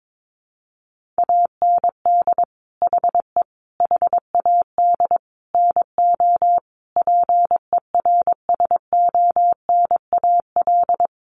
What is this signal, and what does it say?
Morse code, heard as a keyed tone, about 22 wpm, 700 Hz. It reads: AND HE HAD NO PERSONAL
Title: Sentences from Top 1400 Words 22wpm